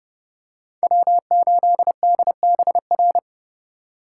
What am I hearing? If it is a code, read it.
Morse code, heard as a keyed tone, about 31 wpm, 700 Hz. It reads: W8DBR